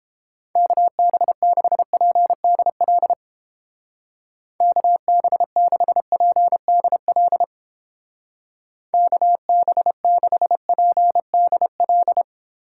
Morse code, heard as a keyed tone, about 30 wpm, 700 Hz. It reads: KB6PDL KB6PDL KB6PDL